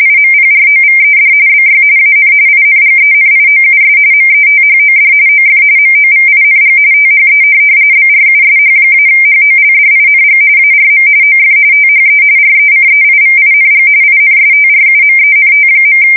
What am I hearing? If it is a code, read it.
RTTY, about 45.45 baud. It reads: IMPROVED BY AN INGENIOUS METHOD OF MOVING THE BRUSH-HOLDING ARM OF THE RECEIVING INSTRUMENT.
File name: fsk170 / printing-telegraph